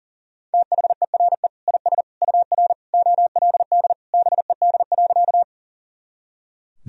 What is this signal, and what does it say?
Morse code, heard as a keyed tone, about 37 wpm, 700 Hz. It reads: THERE IS UR OLD BED.
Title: Sentences from Top 500 Words and Abbreviations 40wpm